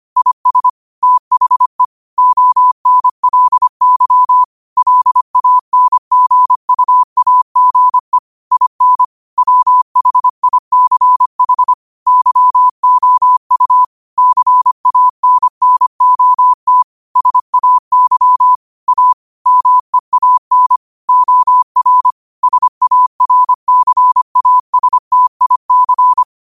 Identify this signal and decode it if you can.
Morse code, heard as a keyed tone, about 25 wpm, 1000 Hz. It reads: IS THE ONLY LANGUAGE IN WHICH YOU CANNOT SAY A MEAN OR SARCASTIC
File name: qotdfiles / 25wpm